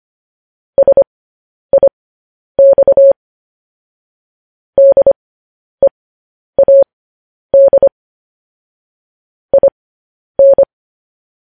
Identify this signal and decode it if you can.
Morse code, heard as a keyed tone, about 25 wpm, spaced as for 10 wpm, 550 Hz. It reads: SIX DEAD IN
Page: news headlines in morse code at 10 wpm